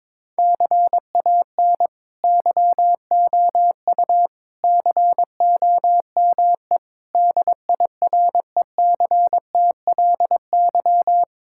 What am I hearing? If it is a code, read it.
Morse code, heard as a keyed tone, about 22 wpm, 700 Hz. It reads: CAN YOU COME DIRECTLY
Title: Sentences from Top 1000 Words - Review Part 5 - 22wpm